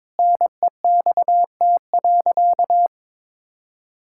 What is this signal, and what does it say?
Morse code, heard as a keyed tone, about 22 wpm, 700 Hz. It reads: NEXT.
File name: practice / 22wpm / Sentences with 8-Word Phrases - 22wpm